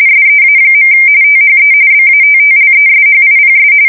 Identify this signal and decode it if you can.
RTTY, about 45.45 baud. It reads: SEA. THE BATTLESHIPS
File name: fsk170 / usnavy5